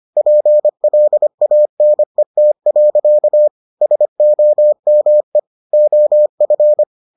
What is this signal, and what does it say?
Morse code, heard as a keyed tone, about 25 wpm, 600 Hz. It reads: PLANET. SOME OF